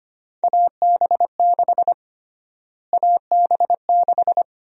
Morse code, heard as a keyed tone, about 25 wpm, 700 Hz. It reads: AB6 AB6
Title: USA Call Sign Prefixes - Repeated 2x - 25wpm